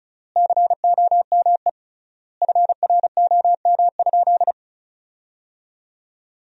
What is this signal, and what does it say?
Morse code, heard as a keyed tone, about 36 wpm, 700 Hz. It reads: COME FROM?
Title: 3x Word-Spacing - Sentences from Top 200 Words - 35wpm